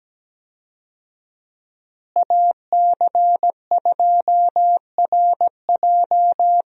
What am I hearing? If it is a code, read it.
Morse code, heard as a keyed tone, about 17 wpm, 700 Hz. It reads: AC2RJ
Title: USA Call Signs - 5 Characters - Repeated 3x - 17wpm